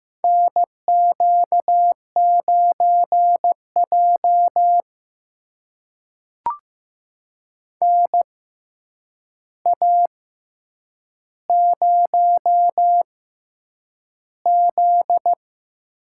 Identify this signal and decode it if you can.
Morse code, heard as a keyed tone, about 15 wpm, spaced as for 8 wpm, 700 Hz. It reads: NQ9J NA0Z
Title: USA Call Signs - 4 Characters - 6x Character Spacing - 15wpm